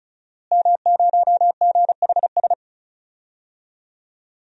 Morse code, heard as a keyed tone, about 36 wpm, 700 Hz. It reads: M0GHS
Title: DXCC-100 Call Signs - 5 Characters - Repeated 2x - 35wpm